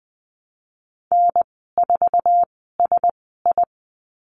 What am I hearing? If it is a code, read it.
Morse code, heard as a keyed tone, about 20 wpm, spaced as for 15 wpm, 700 Hz. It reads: N4SI
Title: 055 Lesson 10g - Letter r - Call Signs 20wpm